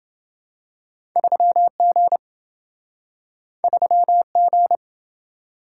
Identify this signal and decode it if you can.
Morse code, heard as a keyed tone, about 27 wpm, 700 Hz. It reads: 3G 3G